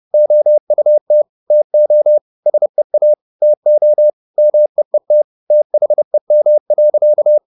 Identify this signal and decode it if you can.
Morse code, heard as a keyed tone, about 30 wpm, 600 Hz. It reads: OUT TO SEA TO MEET THEM.
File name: the old man and the sea / the old man and the sea 1 30wpm